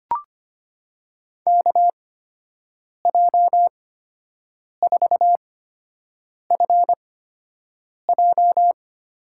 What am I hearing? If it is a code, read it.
Morse code, heard as a keyed tone, about 25 wpm, spaced as for 7 wpm, 700 Hz. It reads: KJ4FJ
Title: USA Call Signs - 8x Character Spacing - 25wpm